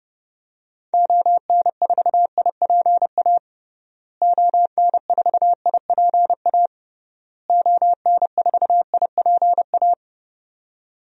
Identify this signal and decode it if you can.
Morse code, heard as a keyed tone, about 30 wpm, 700 Hz. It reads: ON4IPA ON4IPA ON4IPA
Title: DXCC-100 Call Signs - 6 Characters - Repeated 3x - 30wpm